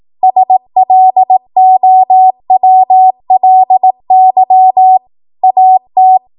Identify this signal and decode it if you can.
Morse code, heard as a keyed tone, about 18 wpm, 750 Hz. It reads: SLOWLY AT
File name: Archive / 18 WPM / 18WPM